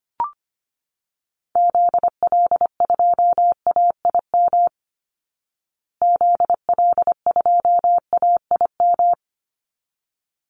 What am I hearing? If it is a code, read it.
Morse code, heard as a keyed tone, about 25 wpm, 700 Hz. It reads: ZL2AIM ZL2AIM
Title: Call Signs Repeated 3x - 25wpm